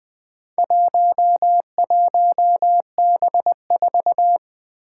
Morse code, heard as a keyed tone, about 20 wpm, 700 Hz. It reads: 11B4
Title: Four Challenging Letter-Number Combinations 20wpm